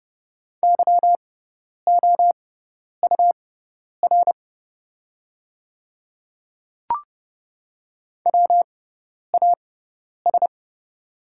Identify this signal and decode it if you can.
Morse code, heard as a keyed tone, about 30 wpm, spaced as for 10 wpm, 700 Hz. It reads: YOUR WAS
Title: 6x Character Spacing - Top 100 Words - 30wpm